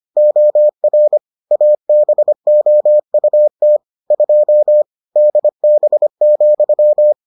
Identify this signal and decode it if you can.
Morse code, heard as a keyed tone, about 25 wpm, 600 Hz. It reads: OR ABOUT 2 DB,